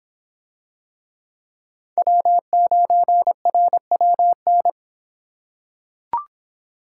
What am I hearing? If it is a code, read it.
Morse code, heard as a keyed tone, about 25 wpm, 700 Hz. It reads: W9RWN